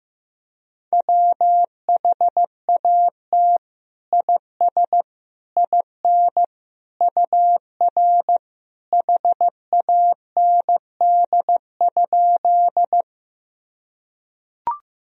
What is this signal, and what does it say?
Morse code, heard as a keyed tone, about 15 wpm, 700 Hz. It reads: WHAT IS IN UR HAND?
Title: Sentences from Top 200 Words and Abbreviations 15wpm